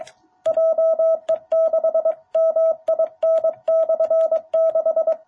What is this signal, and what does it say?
Morse code, heard as a keyed tone, about 23 wpm, 660 Hz. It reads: JE6MIN/6
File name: jh8cbh / 2021cw